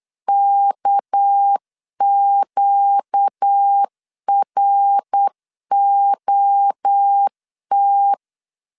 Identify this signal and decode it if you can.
Morse code, heard as a keyed tone, about 8 wpm, 800 Hz. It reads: KQROT
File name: LEZIONI / 6 PKX0 / 6 PK 8wpm 3003